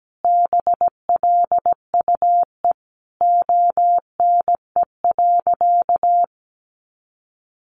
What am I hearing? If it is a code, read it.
Morse code, heard as a keyed tone, about 17 wpm, 700 Hz. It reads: BLUE ONE.
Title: Additional Easy Phrases 17wpm